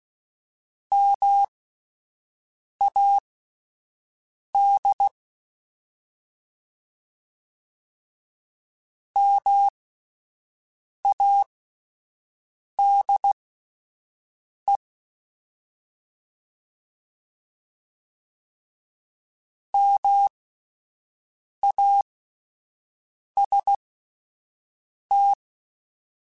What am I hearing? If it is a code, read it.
Morse code, heard as a keyed tone, about 16 wpm, spaced as for 5 wpm, 800 Hz. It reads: MAD MADE MAST
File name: Alphabetical / Alphabetical mp3s / Ms @5